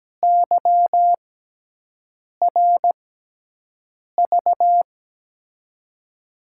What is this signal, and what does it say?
Morse code, heard as a keyed tone, about 17 wpm, spaced as for 6 wpm, 700 Hz. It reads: YRV
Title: USA Call Signs - 6 Characters - 6x Character Spacing - 17wpm